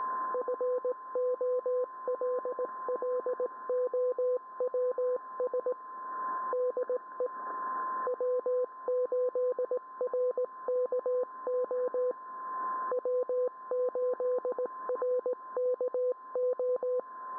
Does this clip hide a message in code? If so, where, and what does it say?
Yes: Morse code from 0.3 to 17.0 s, heard as a keyed tone, about 17 wpm, 500 Hz. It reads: FOLLOWS DE W8RKO W8RKO